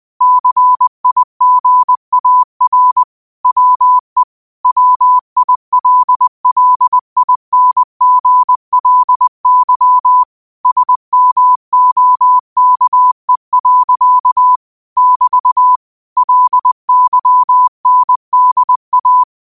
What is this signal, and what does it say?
Morse code, heard as a keyed tone, about 20 wpm, 1000 Hz. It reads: CIGAR WE WILLINGLY SMOKE. <BT> LYNDA